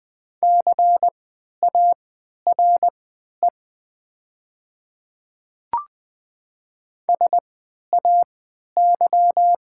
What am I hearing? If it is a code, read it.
Morse code, heard as a keyed tone, about 20 wpm, spaced as for 11 wpm, 700 Hz. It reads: CARE SAY
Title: 3x Character Spacing - Top 300 Words - 20wpm